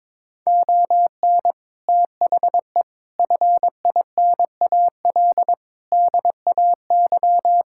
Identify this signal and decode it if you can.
Morse code, heard as a keyed tone, about 22 wpm, 700 Hz. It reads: ON THE FINAL DAY
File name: Sentences with 7-Word Phrases - 22wpm